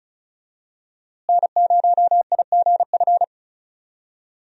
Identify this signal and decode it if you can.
Morse code, heard as a keyed tone, about 36 wpm, 700 Hz. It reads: N0IGF
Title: USA Call Signs - 5 Characters - Repeated 3x - 35wpm